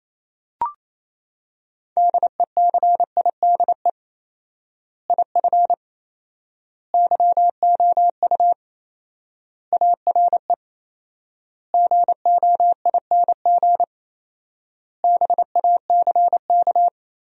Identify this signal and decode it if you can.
Morse code, heard as a keyed tone, about 28 wpm, 700 Hz. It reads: DECIDE IF YOU ARE GOING BACK
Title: 4x Word-Spacing - Sentences from Top 500 Words - 28wpm